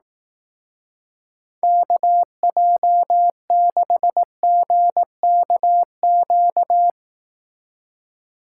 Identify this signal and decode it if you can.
Morse code, heard as a keyed tone, about 18 wpm, 700 Hz. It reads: KJ6GKQ
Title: Speed Racing - USA 2x3 Call Signs 15wpm